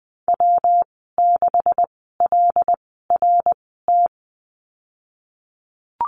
Morse code, heard as a keyed tone, about 20 wpm, spaced as for 15 wpm, 700 Hz. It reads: W6LRT
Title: 140 Lesson 23g - Letter f - Call Signs 20wpm